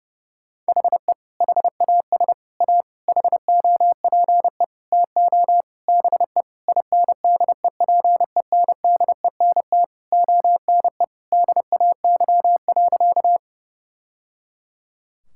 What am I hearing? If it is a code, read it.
Morse code, heard as a keyed tone, about 30 wpm, 700 Hz. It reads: HE HAS A HOPE TO BE INDEPENDENT ONE DAY.
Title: Sentences from Top 1000 Words - Review Part 6 - 30wpm